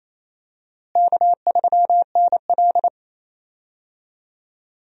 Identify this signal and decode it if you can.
Morse code, heard as a keyed tone, about 28 wpm, 700 Hz. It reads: K3NL
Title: USA Call Signs - 4 Characters - Repeated 3x - 28wpm